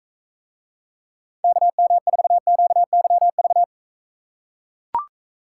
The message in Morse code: KM4QYV

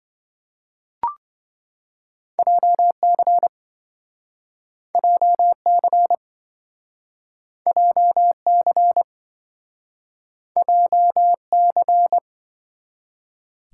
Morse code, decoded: JC JC JC JC

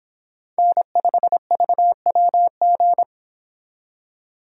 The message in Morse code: N5VWG

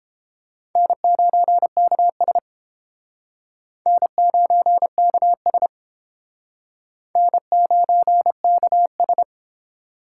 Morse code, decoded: N9KS N9KS N9KS

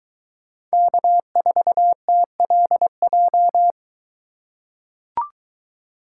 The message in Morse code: K4TLJ

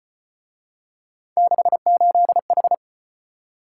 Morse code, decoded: 68H

